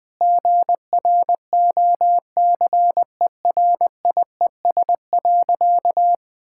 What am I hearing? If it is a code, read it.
Morse code, heard as a keyed tone, about 20 wpm, 700 Hz. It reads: GROCERIES.
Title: Easy Sentences with Hints - 20wpm